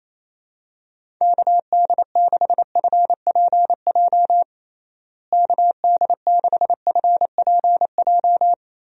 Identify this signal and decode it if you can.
Morse code, heard as a keyed tone, about 27 wpm, 700 Hz. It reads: KD6FPJ KD6FPJ
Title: USA Call Signs - 6 Characters - Repeated 3x - 28wpm